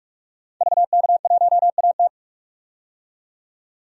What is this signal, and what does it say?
Morse code, heard as a keyed tone, about 48 wpm, 700 Hz. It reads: VK1AT
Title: DXCC-100 Call Signs - 5 Characters - 45wpm